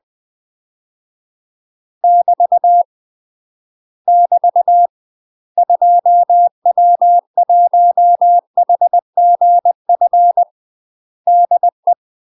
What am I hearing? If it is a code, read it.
Morse code, heard as a keyed tone, about 20 wpm, 700 Hz. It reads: <BT> <BT> 2W1HGF DE
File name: UK MASTER QSOs TRAINING / 20WPM / UK MASTER QSOs 0041